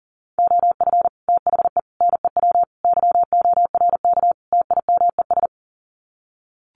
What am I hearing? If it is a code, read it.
Morse code, heard as a keyed tone, about 40 wpm, 700 Hz. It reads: OF THE NEW YORK TIMES
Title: Sets of 5 Words 40wpm